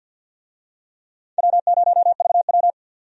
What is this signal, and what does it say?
Morse code, heard as a keyed tone, about 49 wpm, 700 Hz. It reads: W0VW
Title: USA Call Signs - 4 Characters - 50wpm